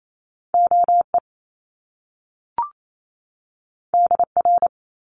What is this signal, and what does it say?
Morse code, heard as a keyed tone, about 28 wpm, 700 Hz. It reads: OE DR
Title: Top Two Letter Combinations in Words 28wpm